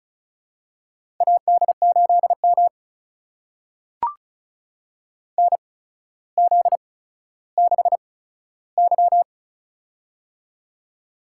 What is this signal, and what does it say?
Morse code, heard as a keyed tone, about 36 wpm, spaced as for 15 wpm, 700 Hz. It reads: AD8M NZ6Y
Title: USA Call Signs - 4 Characters - 8x Character Spacing - 35wpm